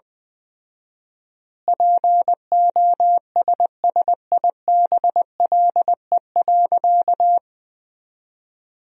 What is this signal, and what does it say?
Morse code, heard as a keyed tone, about 19 wpm, 700 Hz. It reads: POSSIBLE.
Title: 4x Word-Spacing - Sentences from Top 500 Words - 20wpm